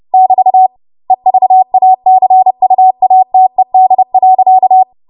Morse code, heard as a keyed tone, about 30 wpm, 750 Hz. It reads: <BT> EVACUATED.
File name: Morse / 30WPM